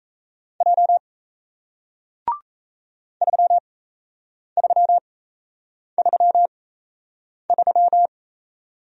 Morse code, heard as a keyed tone, about 38 wpm, spaced as for 8 wpm, 700 Hz. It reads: J 3333